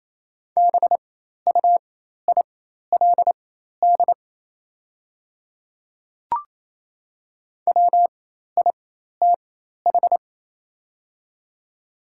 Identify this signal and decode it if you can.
Morse code, heard as a keyed tone, about 28 wpm, spaced as for 13 wpm, 700 Hz. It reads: BUILD WITH
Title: 4x Character Spacing - Top 300 Words - 28wpm